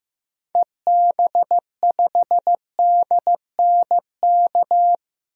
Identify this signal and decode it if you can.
Morse code, heard as a keyed tone, about 15 wpm, 700 Hz. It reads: EB5DNK
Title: DXCC-100 Call Signs - 6 Characters - 15wpm